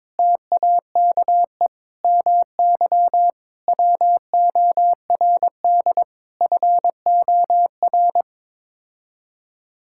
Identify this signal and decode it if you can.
Morse code, heard as a keyed tone, about 22 wpm, 700 Hz. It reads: TAKE MY WORD FOR
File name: practice / 22wpm / Sets of 8 Words - 22wpm